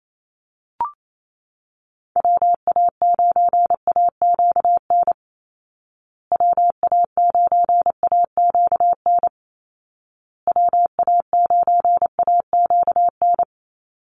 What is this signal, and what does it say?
Morse code, heard as a keyed tone, about 28 wpm, 700 Hz. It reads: WA9AQN WA9AQN WA9AQN
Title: Call Signs Repeated 2x - 28wpm